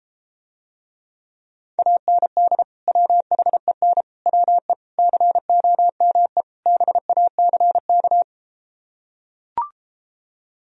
Morse code, heard as a keyed tone, about 33 wpm, 700 Hz. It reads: AND WHEN WE COME BACK